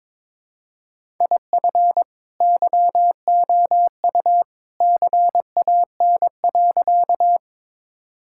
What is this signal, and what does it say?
Morse code, heard as a keyed tone, about 21 wpm, 700 Hz. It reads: IF YOU CAN.
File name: practice / 22wpm / Sentences with 2-Word Phrases - 22wpm